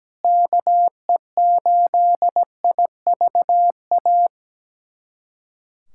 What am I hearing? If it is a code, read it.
Morse code, heard as a keyed tone, about 17 wpm, 700 Hz. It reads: KE8IVA
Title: USA Callsign DE Callsign - 6 Characters - 17wpm